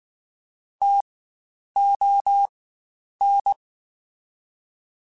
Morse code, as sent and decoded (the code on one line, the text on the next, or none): - --- -.
TON